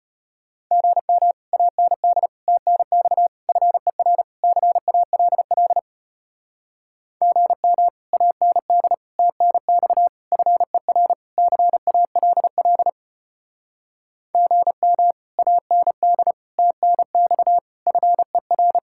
Morse code, heard as a keyed tone, about 33 wpm, 700 Hz. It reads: GM AND TNX FER CALL GM AND TNX FER CALL GM AND TNX FER